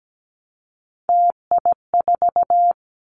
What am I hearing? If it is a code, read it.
Morse code, heard as a keyed tone, about 17 wpm, 700 Hz. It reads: TI4
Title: Callsign Prefixes for Top 100 DXCC - 17wpm